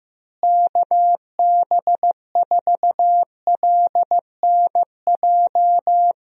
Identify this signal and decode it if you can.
Morse code, heard as a keyed tone, about 15 wpm, 700 Hz. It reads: KB4LNJ
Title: USA Call Signs - 6 Characters - Repeated 3x - 15wpm